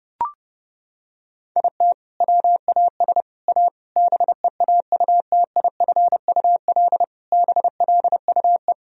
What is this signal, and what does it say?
Morse code, heard as a keyed tone, about 30 wpm, 700 Hz. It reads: IT WAS A BEAUTIFUL BLUE
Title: Sentences from Top 1000 Words - Review Part 3 - 30wpm